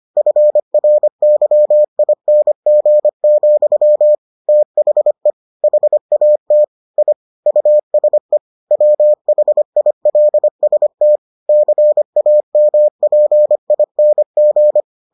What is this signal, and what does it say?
Morse code, heard as a keyed tone, about 25 wpm, 600 Hz. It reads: FRYING, THE HAT I USE WHILST CAMPING